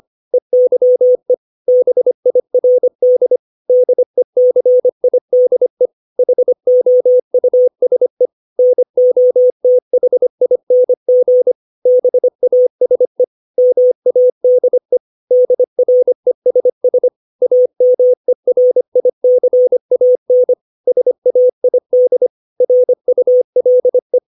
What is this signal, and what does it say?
Morse code, heard as a keyed tone, about 25 wpm, 500 Hz. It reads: EYE BIRD DECIDE HOUSE NOTHING BASE MADE DRESS AMERICAN SAID RULE
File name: 632 words / 632 Words 25WPM